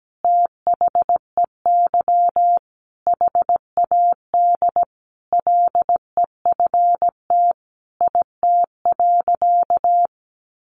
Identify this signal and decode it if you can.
Morse code, heard as a keyed tone, about 17 wpm, 700 Hz. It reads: THEY HAD LEFT IT.
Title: A Sound Of Thunder 17wpm